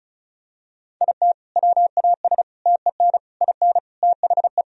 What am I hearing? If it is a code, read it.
Morse code, heard as a keyed tone, about 36 wpm, 700 Hz. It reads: IT WAS TEN IN THE
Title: Sentences from Top 1000 Words - Review Part 2 - 35wpm